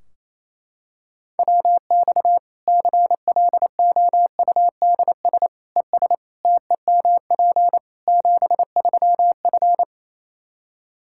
Morse code, heard as a keyed tone, about 28 wpm, 700 Hz. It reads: WX CLOUDS ES TEMP 73F
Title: Practice QSO - Exchange 2 - Info - 28wpm